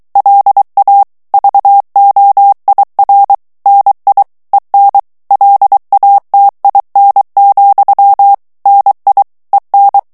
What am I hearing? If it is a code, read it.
Morse code, heard as a keyed tone, about 23 wpm, 800 Hz. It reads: LA VOIR NI EN LATIN, NI EN
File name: CW / FICHIERS-MP3 / FRANCAIS TEXTE CLAIR 2 VITESSE 22 MOTS MINUTES